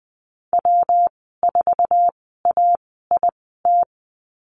Morse code, heard as a keyed tone, about 20 wpm, spaced as for 14 wpm, 700 Hz. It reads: W4AIT
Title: 122 Lesson 20g - Number 3 - Call Signs 20wpm